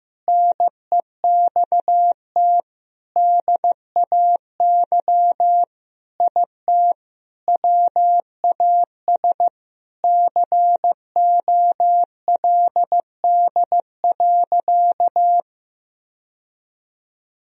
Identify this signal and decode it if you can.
Morse code, heard as a keyed tone, about 15 wpm, 700 Hz. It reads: NEXT DAY IT WAS COLD.
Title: Sentences from Top 1000 Words - Review Part 1 - 15wpm